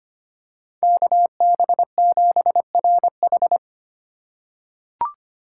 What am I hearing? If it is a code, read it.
Morse code, heard as a keyed tone, about 25 wpm, 700 Hz. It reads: KB7RH
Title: USA Call Signs - 5 Characters - 3x Character Spacing - 25wpm